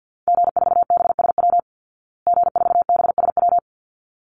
Morse code, heard as a keyed tone, about 50 wpm, 700 Hz. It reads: G4BSW G4BSW
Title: Call Signs Repeated 3x - 50wpm